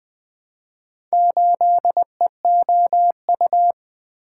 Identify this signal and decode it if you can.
Morse code, heard as a keyed tone, about 20 wpm, 700 Hz. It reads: 8EOU
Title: Four Letter-Number Combinations - 20wpm